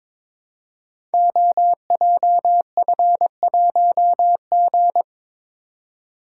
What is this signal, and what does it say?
Morse code, heard as a keyed tone, about 23 wpm, 700 Hz. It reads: OJF1G
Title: Code Groups - 5 Letter:Numbers - 22wpm